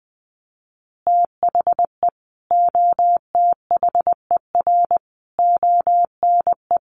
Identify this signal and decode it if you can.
Morse code, heard as a keyed tone, about 20 wpm, 700 Hz. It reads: THE OTHER ONE
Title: Additional Easy Phrases 20wpm